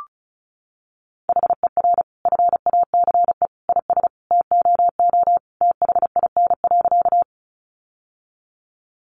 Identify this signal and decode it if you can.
Morse code, heard as a keyed tone, about 34 wpm, 700 Hz. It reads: HER FACE IS TOO THIN.
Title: Sentences from Top 1200 Words 35wpm